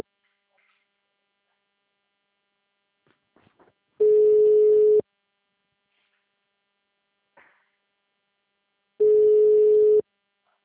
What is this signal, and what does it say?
Ringback tone (Europe, 425 Hz).
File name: out- -2001- - - .62527